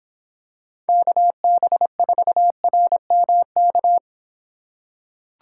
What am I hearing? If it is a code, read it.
Morse code, heard as a keyed tone, about 26 wpm, 700 Hz. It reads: KB4RMK